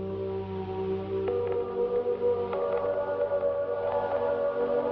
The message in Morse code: none